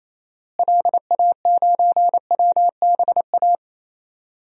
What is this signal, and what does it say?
Morse code, heard as a keyed tone, about 28 wpm, 700 Hz. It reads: LA9WBA
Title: DXCC-100 Call Signs - 6 Characters - Repeated 3x - 28wpm